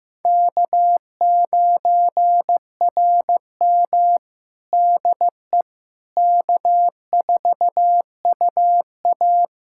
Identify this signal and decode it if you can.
Morse code, heard as a keyed tone, about 15 wpm, 700 Hz. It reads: K9RM DE K4UA
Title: USA Callsign DE Callsign - 4 Characters - 15wpm